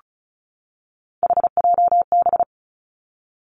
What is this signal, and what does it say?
Morse code, heard as a keyed tone, about 34 wpm, 700 Hz. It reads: HJB
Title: Three Challenging Letter-Number Combinations 35wpm